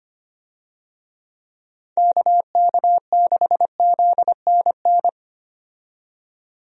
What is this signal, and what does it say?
Morse code, heard as a keyed tone, about 25 wpm, 700 Hz. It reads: KK6ZNN
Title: USA Call Signs - 6 Characters - 25wpm